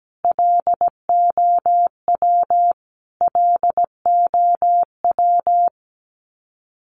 Morse code, heard as a keyed tone, about 17 wpm, 700 Hz. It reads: LOW LOW
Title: Top 200 Words Repeated 17wpm